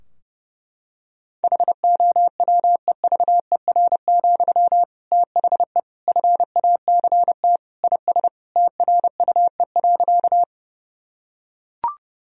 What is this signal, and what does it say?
Morse code, heard as a keyed tone, about 30 wpm, 700 Hz. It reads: HOWEVER, THE FACT IS TRUE.
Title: Sentences from Top 1000 Words - Review Part 3 - 30wpm